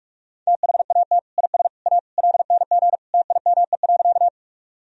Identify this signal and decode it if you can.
Morse code, heard as a keyed tone, about 41 wpm, 700 Hz. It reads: THAT IS A LNG TIME.
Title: Sentences from Top 100 Words and Abbreviations 45wpm